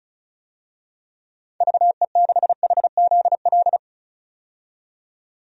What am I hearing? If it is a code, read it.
Morse code, heard as a keyed tone, about 34 wpm, 700 Hz. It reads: VE6HZL
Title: DXCC-100 Call Signs - 6 Characters - 35wpm